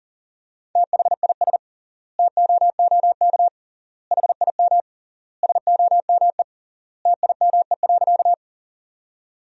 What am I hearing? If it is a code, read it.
Morse code, heard as a keyed tone, about 40 wpm, 700 Hz. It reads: THIS TOOK HIM SOME TIME.
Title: 3x Word-Spacing - Sentences from Top 300 Words - 40wpm